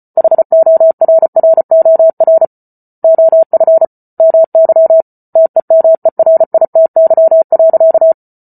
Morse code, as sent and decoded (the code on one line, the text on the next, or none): .... --- .-. .-. --- .-. / --- ..-. / -- -.-- / - . -- . .-. .. - -.-- .-.-.-
HORROR OF MY TEMERITY.